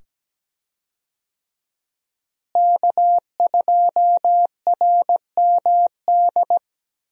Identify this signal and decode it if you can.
Morse code, heard as a keyed tone, about 17 wpm, 700 Hz. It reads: K2RMD
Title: USA Call Signs - 5 Characters - Repeated 2x - 17wpm